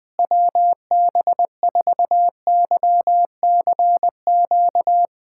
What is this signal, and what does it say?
Morse code, heard as a keyed tone, about 21 wpm, 700 Hz. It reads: WB4YCQ
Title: USA Callsign DE Callsign - 6 Characters - 20wpm